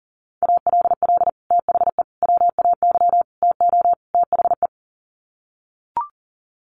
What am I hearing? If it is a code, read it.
Morse code, heard as a keyed tone, about 40 wpm, 700 Hz. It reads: ALL THE WAY TO THE
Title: Sets of 5 Words 40wpm